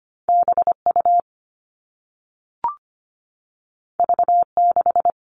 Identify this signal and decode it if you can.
Morse code, heard as a keyed tone, about 25 wpm, 700 Hz. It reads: BU V6
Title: All Cut Numbers - 2-Digit 25wpm